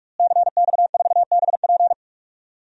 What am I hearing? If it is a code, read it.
Morse code, heard as a keyed tone, about 46 wpm, 700 Hz. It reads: CX4BP
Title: DXCC-100 Call Signs - 5 Characters - Repeated 3x - 45wpm